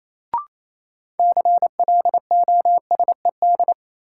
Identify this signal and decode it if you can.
Morse code, heard as a keyed tone, about 28 wpm, 700 Hz. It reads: CLOSED
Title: Sentences from Top 1400 Words - New Word Drill - 28wpm